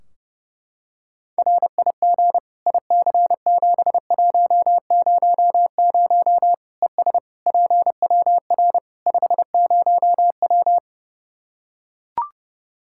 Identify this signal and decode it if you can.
Morse code, heard as a keyed tone, about 30 wpm, 700 Hz. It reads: RIG IC7100 ES PWR 50W
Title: Practice QSO - Exchange 2 - Info - 30wpm